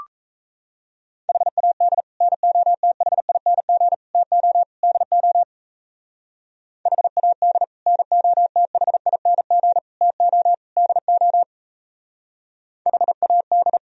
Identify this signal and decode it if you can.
Morse code, heard as a keyed tone, about 38 wpm, 700 Hz. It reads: HAD NOTHING TO DO HAD NOTHING TO DO HAD